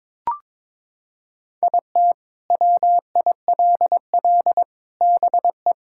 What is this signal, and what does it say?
Morse code, heard as a keyed tone, about 21 wpm, 700 Hz. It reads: IT WILL BE
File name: practice / 22wpm / Sentences with 8-Word Phrases - 22wpm